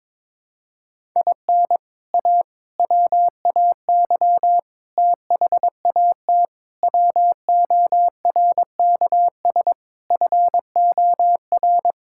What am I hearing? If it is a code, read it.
Morse code, heard as a keyed tone, about 22 wpm, 700 Hz. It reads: IN A WAY THAT WORKS FOR
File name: practice / 22wpm / Sentences with 4-Word Phrases - 22wpm